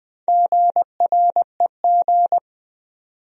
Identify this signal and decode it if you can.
Morse code, heard as a keyed tone, about 21 wpm, 700 Hz. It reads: GREG
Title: USA Common Names - Max 4 Letters - 20wpm